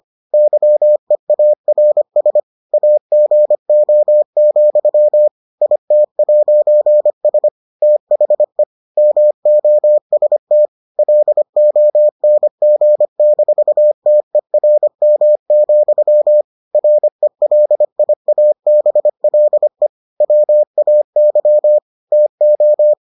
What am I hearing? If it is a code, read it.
Morse code, heard as a keyed tone, about 25 wpm, 600 Hz. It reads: YEARS AGO, IT'S THE MOST LONG-TERM, RELIABLE WAY TO